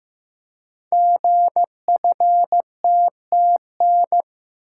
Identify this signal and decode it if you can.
Morse code, heard as a keyed tone, about 15 wpm, 700 Hz. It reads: GFTTN
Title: Code Groups - 5 Letters - 15wpm